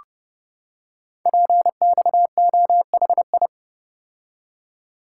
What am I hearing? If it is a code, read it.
Morse code, heard as a keyed tone, about 30 wpm, 700 Hz. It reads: PXOHI
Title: Code Groups - 5 Letters - 30wpm